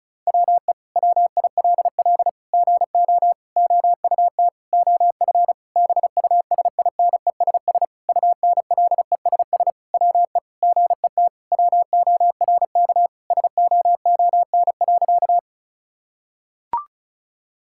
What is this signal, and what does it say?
Morse code, heard as a keyed tone, about 36 wpm, 700 Hz. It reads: WE WILL GO OUT OF BUSINESS UNLESS WE GET WORK SOON.
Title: Sentences from Top 1000 Words - Review Part 6 - 35wpm